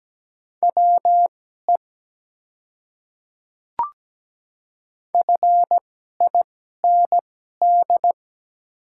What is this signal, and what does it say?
Morse code, heard as a keyed tone, about 17 wpm, spaced as for 12 wpm, 700 Hz. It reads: WE FIND
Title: 2x Character Spacing - Top 200 Words - 17wpm